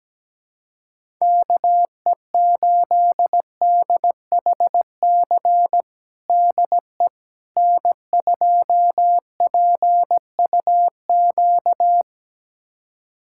Morse code, heard as a keyed tone, about 17 wpm, 700 Hz. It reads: KE8DHC DE N2PUQ
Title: USA Callsign DE Callsign 17wpm